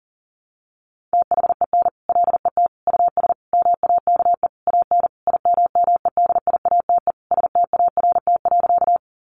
Morse code, heard as a keyed tone, about 40 wpm, 700 Hz. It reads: THEN LET US MAKE AN IMMEDIATE START.
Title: Sentences from Top 1200 Words 40wpm